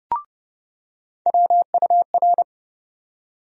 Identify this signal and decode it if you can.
Morse code, heard as a keyed tone, about 32 wpm, 700 Hz. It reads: WUR
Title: USA Call Sign Postfixes - 3 Letters 30wpm